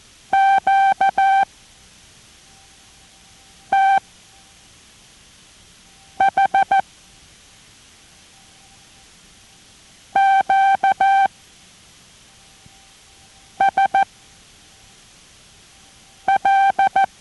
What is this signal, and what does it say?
Morse code, heard as a keyed tone, about 14 wpm, spaced as for 4 wpm, 780 Hz. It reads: QTH QSL